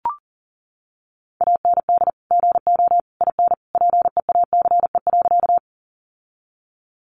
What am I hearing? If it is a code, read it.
Morse code, heard as a keyed tone, about 40 wpm, 700 Hz. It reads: AND GO IN PEACE.
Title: Sentences from Top 1500 Words 40wpm